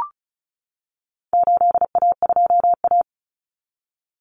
Morse code, heard as a keyed tone, about 34 wpm, 700 Hz. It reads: 8A2A
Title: Common Cut Numbers - 4-Digit 35wpm